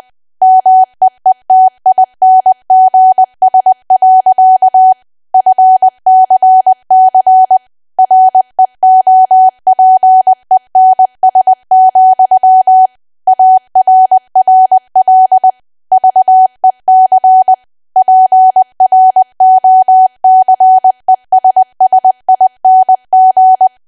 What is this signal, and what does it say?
Morse code, heard as a keyed tone, about 20 wpm, 750 Hz. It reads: MEETINGS. FCC REOPENS, ARRL VEC PROCESSING
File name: cwa / W1AW-Level3 / 20WPM